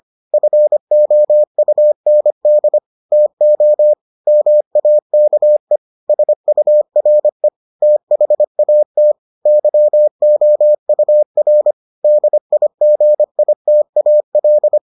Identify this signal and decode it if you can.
Morse code, heard as a keyed tone, about 25 wpm, 600 Hz. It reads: FOUND TO MAKE SURE THAT YOUR DIGITAL